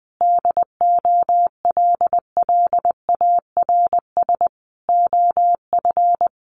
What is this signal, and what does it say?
Morse code, heard as a keyed tone, about 20 wpm, 700 Hz. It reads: DOLLARS OF
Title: A Sound Of Thunder 20wpm